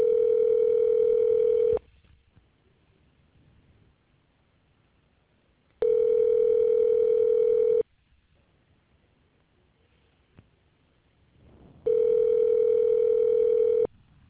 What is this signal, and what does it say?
Ringback tone (North America), 440 + 480 Hz.